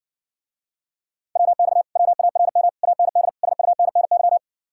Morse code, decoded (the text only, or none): WX WARM AND SUNNY